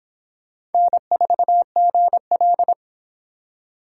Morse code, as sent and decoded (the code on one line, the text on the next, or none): -. ....- --. .-..
N4GL